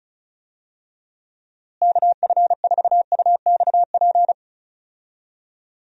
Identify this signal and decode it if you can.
Morse code, heard as a keyed tone, about 34 wpm, 700 Hz. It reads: KF4UXP
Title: USA Call Signs - 6 Characters - Repeated 3x - 35wpm